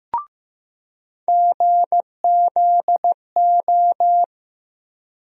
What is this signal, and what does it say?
Morse code, heard as a keyed tone, about 15 wpm, 700 Hz. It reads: GZO
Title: USA Call Sign Postfixes - 3 Letters 15wpm